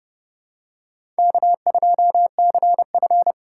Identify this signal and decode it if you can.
Morse code, heard as a keyed tone, about 30 wpm, 700 Hz. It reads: K2CF
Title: USA Call Signs - 4 Characters - 2x Character Spacing - 30wpm